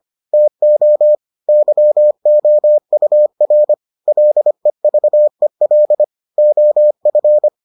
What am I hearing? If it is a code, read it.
Morse code, heard as a keyed tone, about 25 wpm, 600 Hz. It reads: TO YOUR LEVEL OF